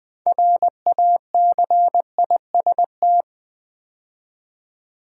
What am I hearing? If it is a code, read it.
Morse code, heard as a keyed tone, about 20 wpm, 700 Hz. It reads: RACIST
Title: 6 Letter Words - First 3 Letters Given - 20wpm